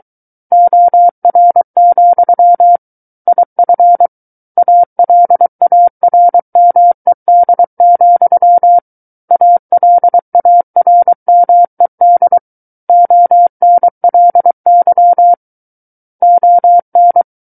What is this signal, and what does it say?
Morse code, heard as a keyed tone, about 23 wpm, 700 Hz. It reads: OR, IF ALARMED, ALARMED ONLY ON
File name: Book One / 14-Chapter 14 - 23 WPM